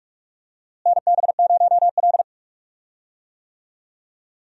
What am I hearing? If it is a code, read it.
Morse code, heard as a keyed tone, about 47 wpm, 700 Hz. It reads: NB0L
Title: USA Call Signs - 4 Characters - Repeated 2x - 45wpm